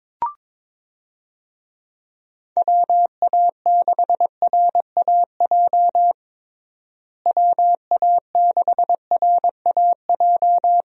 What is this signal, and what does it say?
Morse code, heard as a keyed tone, about 22 wpm, 700 Hz. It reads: WA6RAJ WA6RAJ
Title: USA Call Signs - 6 Characters - Repeated 2x - 22wpm